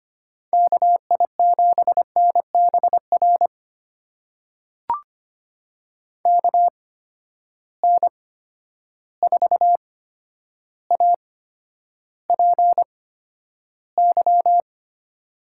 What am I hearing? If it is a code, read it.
Morse code, heard as a keyed tone, about 25 wpm, spaced as for 11 wpm, 700 Hz. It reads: KI7NBR KN4APY
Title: USA Call Signs - 6 Characters - 8x Character Spacing - 25wpm